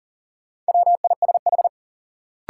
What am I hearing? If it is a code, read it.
Morse code, heard as a keyed tone, about 40 wpm, 700 Hz. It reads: WISH
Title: Sentences from Top 1600 Words - New Word Drill Alt - 40wpm